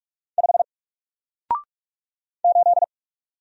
Morse code, heard as a keyed tone, about 44 wpm, 700 Hz. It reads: 5 8
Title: Single Letter-Number Alt 45wpm